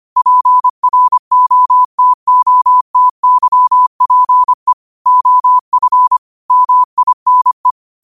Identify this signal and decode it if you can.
Morse code, heard as a keyed tone, about 25 wpm, 1000 Hz. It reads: PROTOTYPE OF MINE
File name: qotdfiles / 25wpm